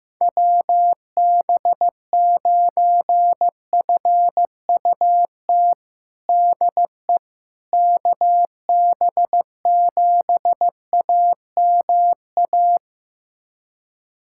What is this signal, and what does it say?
Morse code, heard as a keyed tone, about 16 wpm, 700 Hz. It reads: WB9FUT DE KB7AMA
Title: USA Callsign DE Callsign - 6 Characters - 15wpm